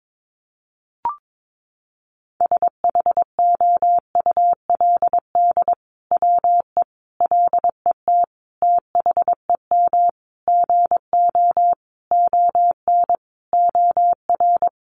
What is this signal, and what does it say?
Morse code, heard as a keyed tone, about 22 wpm, 700 Hz. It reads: SHOULD WE LET THEM GO ON OR
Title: Sentences from Top 600 Words 22wpm